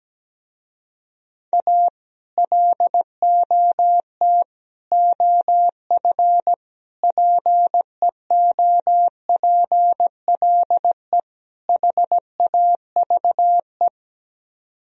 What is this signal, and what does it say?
Morse code, heard as a keyed tone, about 17 wpm, 700 Hz. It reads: A LOT OF PEOPLE HAVE
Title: Speed-Racing - Sets of 5 Words 17wpm